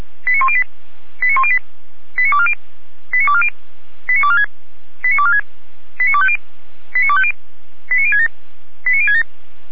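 Five-tone selective call (ZVEI), heard as quick runs of five tones, tones of about 70 ms. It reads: 89108 89108 89240 89240 89246 89246 89250 89250 89067 89067
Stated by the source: ZVEI five-tone call received by SDR, from the feuerwehr-orlamuende.org fire brigade audio archive.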